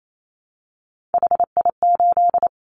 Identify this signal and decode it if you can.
Morse code, heard as a keyed tone, about 28 wpm, 700 Hz. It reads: HI8
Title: Callsign Prefixes for Top 100 DXCC - 28wpm